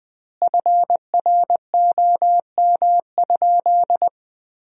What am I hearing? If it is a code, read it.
Morse code, heard as a keyed tone, about 21 wpm, 700 Hz. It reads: FROM?
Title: 2x Word-Spacing - Sentences from Top 200 Words - 20wpm